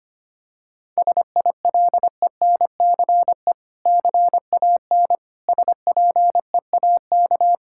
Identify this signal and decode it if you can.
Morse code, heard as a keyed tone, about 25 wpm, 700 Hz. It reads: SILENCE CAN SPEAK
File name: practice / 25wpm / Difficult Sentences with Hints - 25wpm